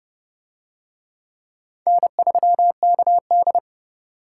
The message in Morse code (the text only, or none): N3KD